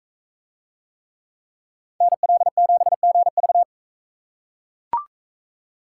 Morse code, heard as a keyed tone, about 40 wpm, 700 Hz. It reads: NL7GV